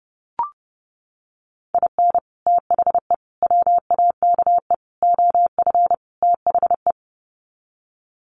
Morse code, heard as a keyed tone, about 30 wpm, 700 Hz. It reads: IN THE WAKE OF THE
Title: Sets of 5 Words 30wpm